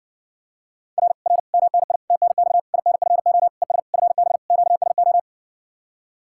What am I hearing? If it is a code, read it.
Morse code, heard as a keyed tone, about 60 wpm, 700 Hz. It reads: R R MNI TNX INFO ES FB QSO